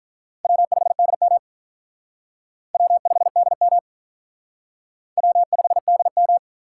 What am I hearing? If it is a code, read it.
Morse code, heard as a keyed tone, about 43 wpm, 700 Hz. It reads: W5DM W5DM W5DM
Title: Speed Racing - USA Call Signs 35wpm